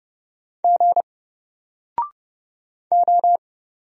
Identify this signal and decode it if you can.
Morse code, heard as a keyed tone, about 30 wpm, 700 Hz. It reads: G O
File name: Single-Letter-Number-30wpm